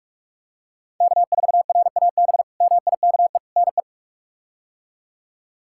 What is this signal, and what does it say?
Morse code, heard as a keyed tone, about 46 wpm, 700 Hz. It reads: K4RAB MIKE NE
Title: K1USN SST QSO Practice - 45wpm